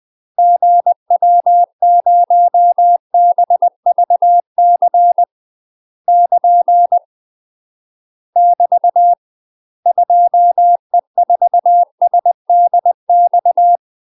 Morse code, heard as a keyed tone, about 20 wpm, 700 Hz. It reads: GW0BVC <KN> <BT> 2E4SDX
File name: MASTER QSOs TRAINING/20WPM